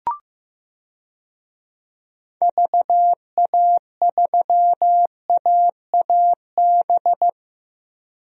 Morse code, heard as a keyed tone, about 15 wpm, 700 Hz. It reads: VA3AAB
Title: DXCC-100 Call Signs - 6 Characters - Repeated 3x - 15wpm